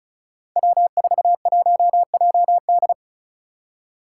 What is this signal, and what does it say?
Morse code, heard as a keyed tone, about 36 wpm, 700 Hz. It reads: W41JD
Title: Five Challenging Letter-Number Combinations 35wpm